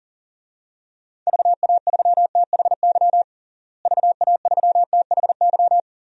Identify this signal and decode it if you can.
Morse code, heard as a keyed tone, about 40 wpm, 700 Hz. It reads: VA3THY VA3THY
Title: DXCC-100 Call Signs - 6 Characters - Repeated 3x - 40wpm